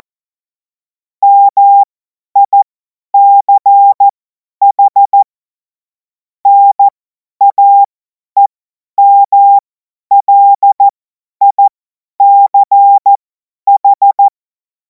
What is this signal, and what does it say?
Morse code, heard as a keyed tone, about 14 wpm, spaced as for 10 wpm, 800 Hz. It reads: MICH NAEMLICH
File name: Morsekurs / Witze / Witz 17. März 2026 50 F70